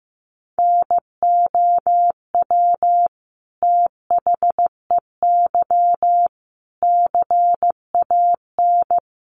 Morse code, heard as a keyed tone, about 15 wpm, 700 Hz. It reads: NOW THEY CAN
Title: Sentences from Top 700 Words 15wpm